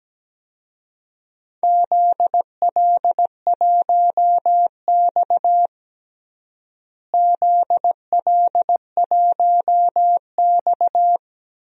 Morse code, heard as a keyed tone, about 17 wpm, 700 Hz. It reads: ZL1X ZL1X
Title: DXCC-100 Call Signs - 4 Characters - Repeated 3x - 17wpm